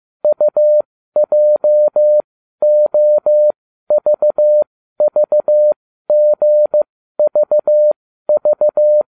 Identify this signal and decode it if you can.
Morse code, heard as a keyed tone, about 15 wpm, spaced as for 12 wpm, 600 Hz. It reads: UJOVVGVV